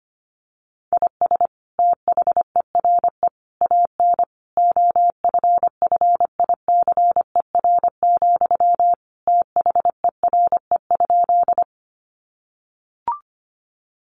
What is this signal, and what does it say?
Morse code, heard as a keyed tone, about 25 wpm, 700 Hz. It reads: IS THERE AN OFFICER, THERE?
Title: Sentences from Top 1200 Words 25wpm